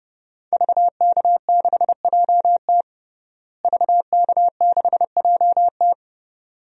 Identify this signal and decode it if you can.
Morse code, heard as a keyed tone, about 30 wpm, 700 Hz. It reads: VK6JT VK6JT
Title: DXCC-100 Call Signs - 5 Characters - Repeated 2x - 30wpm